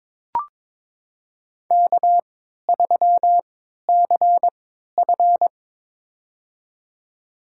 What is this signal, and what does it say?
Morse code, heard as a keyed tone, about 22 wpm, spaced as for 13 wpm, 700 Hz. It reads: K3CF
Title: USA Call Signs - 4 Characters - 3x Character Spacing - 22wpm